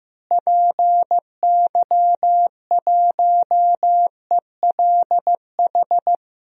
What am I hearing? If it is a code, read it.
Morse code, heard as a keyed tone, about 15 wpm, 700 Hz. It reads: PY1ELH
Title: DXCC-100 Call Signs - 6 Characters - Repeated 3x - 15wpm